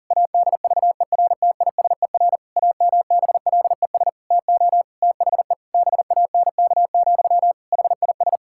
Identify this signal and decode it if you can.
Morse code, heard as a keyed tone, about 40 wpm, 700 Hz. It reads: ADVERTISER AMBLES TO THE BANK, HIS
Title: Pangram Challenge v2 - 40wpm